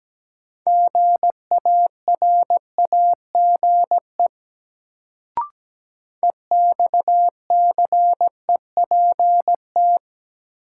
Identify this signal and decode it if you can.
Morse code, heard as a keyed tone, about 17 wpm, 700 Hz. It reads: GARAGE EXCEPT
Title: Sentences from Top 1600 Words - New Word Drill Alt - 17wpm